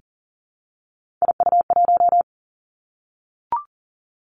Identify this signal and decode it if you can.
Morse code, heard as a keyed tone, about 42 wpm, 700 Hz. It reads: IU1
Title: Three Challenging Letter-Number Combinations 40wpm